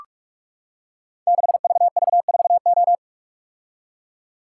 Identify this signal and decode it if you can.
Morse code, heard as a keyed tone, about 44 wpm, 700 Hz. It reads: 6VV4O
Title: Code Groups - 5 Letter:Numbers - 45wpm